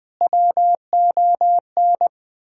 Morse code, heard as a keyed tone, about 21 wpm, 700 Hz. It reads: WON
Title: Sentences with 8-Word Phrases - 20wpm